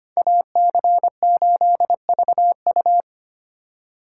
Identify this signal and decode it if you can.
Morse code, heard as a keyed tone, about 32 wpm, 700 Hz. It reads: AC8VU